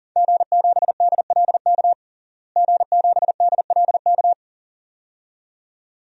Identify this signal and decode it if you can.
Morse code, heard as a keyed tone, about 40 wpm, 700 Hz. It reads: G7DLK G7DLK
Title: DXCC-100 Call Signs - 5 Characters - Repeated 2x - 40wpm